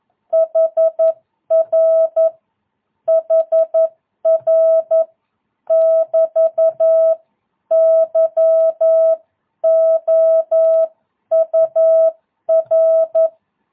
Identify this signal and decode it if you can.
Morse code, heard as a keyed tone, about 11 wpm, 650 Hz. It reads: HR HR <BT> YOUR
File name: jh8cbh / 2022cw